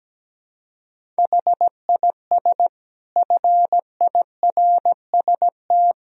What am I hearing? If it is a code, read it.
Morse code, heard as a keyed tone, about 17 wpm, 700 Hz. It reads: HIS FIRST
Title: Sentences from Top 500 Words and Abbreviations 17wpm